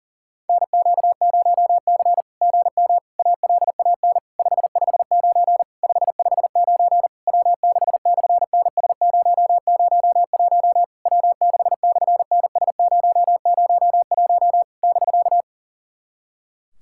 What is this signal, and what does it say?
Morse code, heard as a keyed tone, about 40 wpm, 690 Hz. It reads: NQ0C GM ALAN 559 559 W6/NS001 W6/NS001 <BK>
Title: SOTA QSO Practice 40wpm